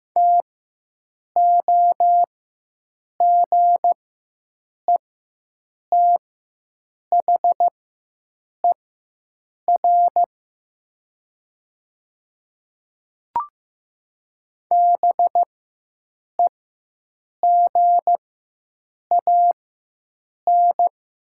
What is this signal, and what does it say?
Morse code, heard as a keyed tone, about 15 wpm, spaced as for 7 wpm, 700 Hz. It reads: TOGETHER BEGAN
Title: 4x Character Spacing - Top 300 Words - 15wpm